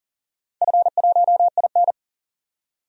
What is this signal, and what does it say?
Morse code, heard as a keyed tone, about 40 wpm, 700 Hz. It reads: F1IN
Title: DXCC-100 Call Signs - 4 Characters - 40wpm